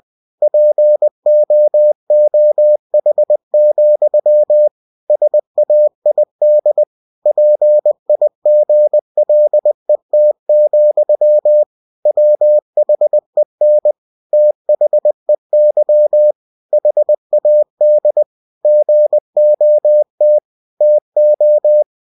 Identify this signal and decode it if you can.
Morse code, heard as a keyed tone, about 20 wpm, 600 Hz. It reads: POOH, SAID PIGLET, WHEN THEY HAD GOT TO
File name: cw-books / winnie-the-pooh / winnie-20wpm-page-0038